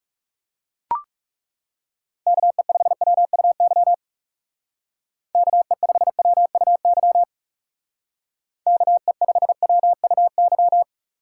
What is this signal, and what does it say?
Morse code, heard as a keyed tone, about 36 wpm, 700 Hz. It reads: KE5WUY KE5WUY KE5WUY
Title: Speed Racing - USA 2x3 Call Signs 30wpm